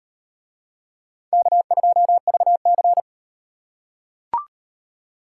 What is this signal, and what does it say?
Morse code, heard as a keyed tone, about 33 wpm, 700 Hz. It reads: K2VC